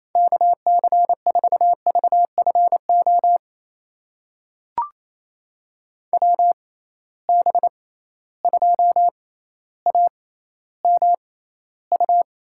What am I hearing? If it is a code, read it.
Morse code, heard as a keyed tone, about 27 wpm, spaced as for 14 wpm, 700 Hz. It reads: KC4VFO WB2AMU
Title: USA Call Signs - 6 Characters - 6x Character Spacing - 28wpm